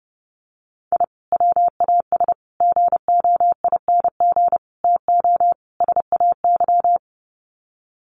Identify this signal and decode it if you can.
Morse code, heard as a keyed tone, about 30 wpm, 700 Hz. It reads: I WAS GOING TO SAY
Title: Sets of 5 Words 30wpm